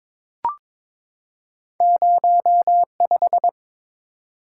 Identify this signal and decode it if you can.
Morse code, heard as a keyed tone, about 22 wpm, 700 Hz. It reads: 05
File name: Two Letter-Number Combinations - 22wpm